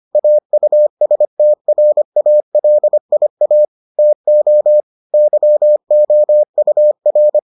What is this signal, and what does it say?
Morse code, heard as a keyed tone, about 25 wpm, 600 Hz. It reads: AUSTRALIA TO YOUR